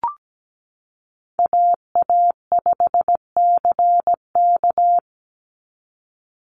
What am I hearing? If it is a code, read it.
Morse code, heard as a keyed tone, about 17 wpm, 700 Hz. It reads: AA5CK
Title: NA SOTA Chaser Callsigns 17wpm